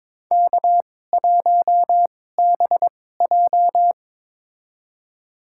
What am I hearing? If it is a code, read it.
Morse code, heard as a keyed tone, about 21 wpm, spaced as for 16 wpm, 700 Hz. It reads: K1BJ
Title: USA Call Signs - 4 Characters - 2x Character Spacing - 22wpm